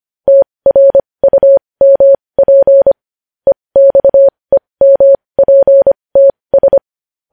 Morse code, heard as a keyed tone, about 25 wpm, spaced as for 20 wpm, 550 Hz. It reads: TRUMP EXEMPTS
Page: News Headlines in Morse Code at 20 WPM